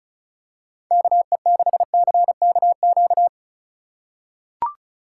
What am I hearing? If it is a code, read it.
Morse code, heard as a keyed tone, about 34 wpm, 700 Hz. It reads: KE6CKQ
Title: USA Call Signs - 6 Characters - 3x Character Spacing - 35wpm